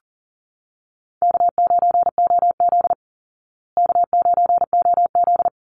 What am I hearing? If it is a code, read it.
Morse code, heard as a keyed tone, about 40 wpm, 700 Hz. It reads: K9OZ K9OZ
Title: Call Signs Repeated 3x - 40wpm